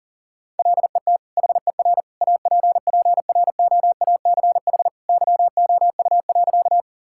Morse code, heard as a keyed tone, about 40 wpm, 700 Hz. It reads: LET HER APPROACH YOU.
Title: Sentences from Top 1000 Words - Review Part 6 - 40wpm